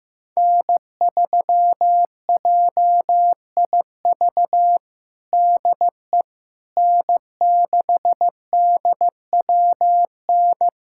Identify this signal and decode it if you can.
Morse code, heard as a keyed tone, about 16 wpm, 700 Hz. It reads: N3JIV DE N6DWN
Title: USA Callsign DE Callsign - 5 Characters - 15wpm